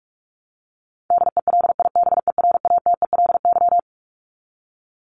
Morse code, heard as a keyed tone, about 44 wpm, 700 Hz. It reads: DELIBERATELY
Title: Top 1751-2000 Words 45wpm